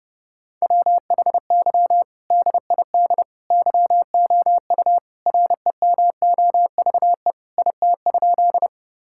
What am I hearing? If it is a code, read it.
Morse code, heard as a keyed tone, about 32 wpm, 700 Hz. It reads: WHY DID YOU REMOVE IT?
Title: Sentences from Top 1000 Words - Review Part 5 - 30wpm